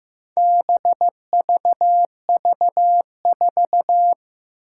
Morse code, heard as a keyed tone, about 15 wpm, 700 Hz. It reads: BVV4
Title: Four Challenging Letter-Number Combinations 15wpm